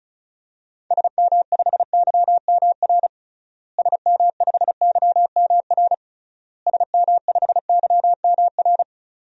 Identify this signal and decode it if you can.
Morse code, heard as a keyed tone, about 36 wpm, 700 Hz. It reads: SM5YMR SM5YMR SM5YMR
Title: DXCC-100 Call Signs - 6 Characters - Repeated 3x - 35wpm